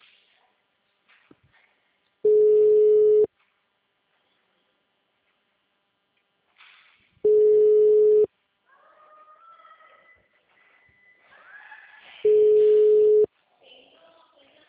Ringback tone (Europe, 425 Hz).